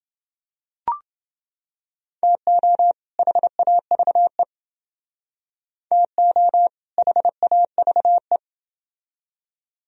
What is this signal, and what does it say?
Morse code, heard as a keyed tone, about 29 wpm, 700 Hz. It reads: TO HAVE TO HAVE